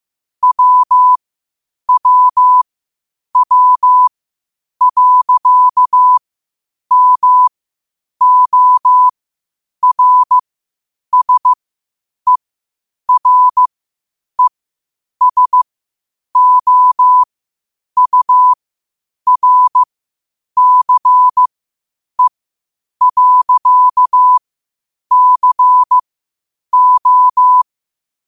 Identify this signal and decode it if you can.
Morse code, heard as a keyed tone, about 15 wpm, spaced as for 8 wpm, 1000 Hz. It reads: WWW.MORSERESOURCE.CO
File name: qotdfiles / 8wpm